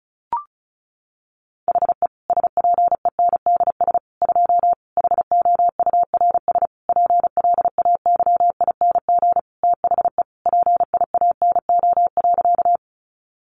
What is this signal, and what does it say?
Morse code, heard as a keyed tone, about 34 wpm, 700 Hz. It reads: HE SPENDS 2 HOURS PLAYING THE PIANO.
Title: Additional Easy Phrases 35wpm